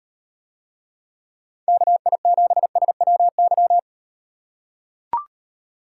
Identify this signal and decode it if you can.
Morse code, heard as a keyed tone, about 36 wpm, 700 Hz. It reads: KI7SWY